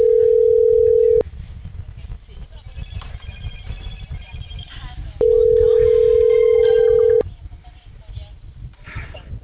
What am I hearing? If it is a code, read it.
Ringback tone (North America), 440 + 480 Hz.